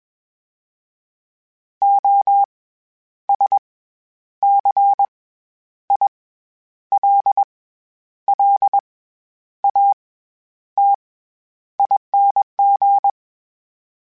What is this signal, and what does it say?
Morse code, heard as a keyed tone, about 21 wpm, spaced as for 10 wpm, 800 Hz. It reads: OSCILLATING
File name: Special INGs-mp3s / Special INGs-mp3s / SlowToFaster ing @20